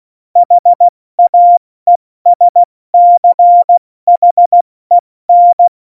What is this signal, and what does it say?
Morse code, heard as a keyed tone, about 16 wpm, 700 Hz. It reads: HAESCHEN
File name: Morsekurs